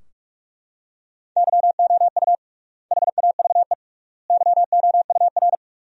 Morse code, heard as a keyed tone, about 41 wpm, 700 Hz. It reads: YOU HAVE YOUR
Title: 3x Word-Spacing - Sentences from Top 400 Words - 45wpm